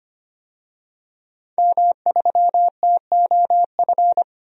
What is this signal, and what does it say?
Morse code, heard as a keyed tone, about 25 wpm, 700 Hz. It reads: M3TOF
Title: DXCC-100 Call Signs - 5 Characters - 25wpm